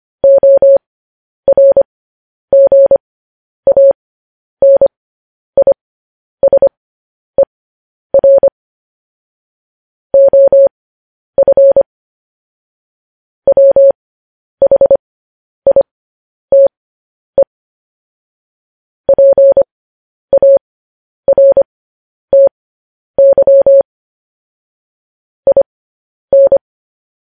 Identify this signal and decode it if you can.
Morse code, heard as a keyed tone, about 25 wpm, spaced as for 10 wpm, 550 Hz. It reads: ORGANISER OF WHITE PARTY IN